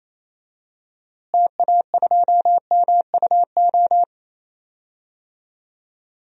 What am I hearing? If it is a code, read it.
Morse code, heard as a keyed tone, about 28 wpm, 700 Hz. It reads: TA2MUO
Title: DXCC-100 Call Signs - 6 Characters - Repeated 3x - 28wpm